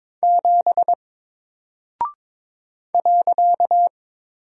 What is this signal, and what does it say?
Morse code, heard as a keyed tone, about 22 wpm, 700 Hz. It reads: 7 .
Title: Single Character (Letter:Number:Punctuation:Prosign) 22wpm